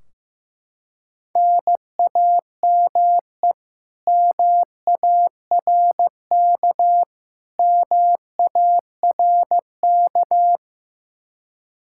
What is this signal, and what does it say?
Morse code, heard as a keyed tone, about 15 wpm, 700 Hz. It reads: NAME MARK MARK
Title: Practice QSO - Exchange 1 - Beginning - 15wpm